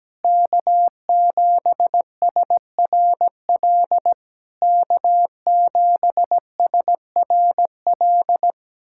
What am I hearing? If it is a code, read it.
Morse code, heard as a keyed tone, about 17 wpm, 690 Hz. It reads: K7SRL K7SRL
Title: SOTA QSO Practice 17wpm